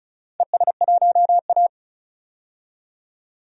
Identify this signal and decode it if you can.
Morse code, heard as a keyed tone, about 36 wpm, 700 Hz. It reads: ES1A
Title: DXCC-100 Call Signs - 4 Characters - 35wpm